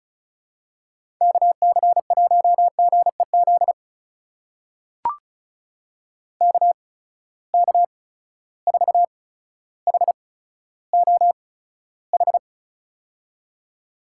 Morse code, heard as a keyed tone, about 36 wpm, spaced as for 15 wpm, 700 Hz. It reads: KC1GEZ KK4HOH
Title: USA Call Signs - 6 Characters - 8x Character Spacing - 35wpm